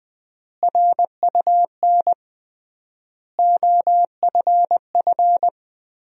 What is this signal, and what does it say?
Morse code, heard as a keyed tone, about 20 wpm, 700 Hz. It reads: RUN OFF
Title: 3x Word-Spacing - Sentences from Top 300 Words - 20wpm